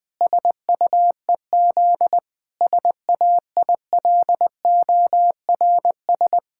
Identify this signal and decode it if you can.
Morse code, heard as a keyed tone, about 20 wpm, 700 Hz. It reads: SUEZ SAILORS
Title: Pangram Challenge v2 - 20wpm